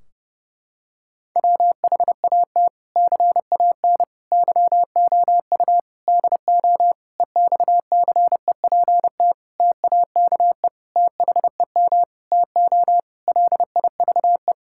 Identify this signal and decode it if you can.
Morse code, heard as a keyed tone, about 30 wpm, 700 Hz. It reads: WHAT CAN YOU DO EXCEPT TAKE THEM TO LIVE
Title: Sentences from Top 1600 Words 30wpm